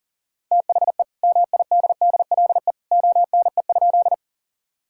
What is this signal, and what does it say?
Morse code, heard as a keyed tone, about 37 wpm, 700 Hz. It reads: THE MIDDLE ONE?
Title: Sentences from Top 1000 Words - Review Part 6 - 40wpm